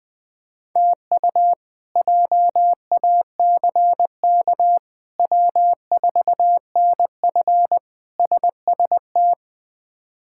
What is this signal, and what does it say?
Morse code, heard as a keyed tone, about 21 wpm, 700 Hz. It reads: TU JACK W4NF SST
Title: K1USN SST QSO Practice - 20wpm